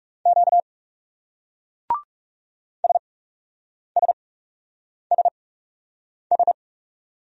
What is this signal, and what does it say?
Morse code, heard as a keyed tone, about 39 wpm, spaced as for 8 wpm, 700 Hz. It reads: Q SSSS